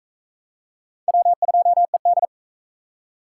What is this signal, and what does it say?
Morse code, heard as a keyed tone, about 40 wpm, 700 Hz. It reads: W2ED